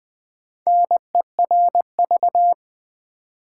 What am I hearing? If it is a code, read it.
Morse code, heard as a keyed tone, about 20 wpm, 700 Hz. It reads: NERV
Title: Build A Word - 1 Letter At A Time - 3 to 5-Letter Words - 20wpm